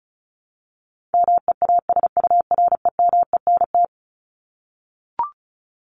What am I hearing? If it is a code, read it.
Morse code, heard as a keyed tone, about 34 wpm, 700 Hz. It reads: MEASUREMENT
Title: Top 1751-2000 Words 35wpm